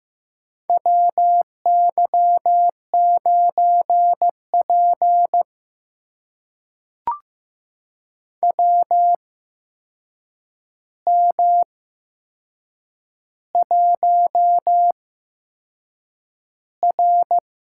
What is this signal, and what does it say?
Morse code, heard as a keyed tone, about 15 wpm, spaced as for 6 wpm, 700 Hz. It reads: WY9P WM1R
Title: USA Call Signs - 4 Characters - 8x Character Spacing - 15wpm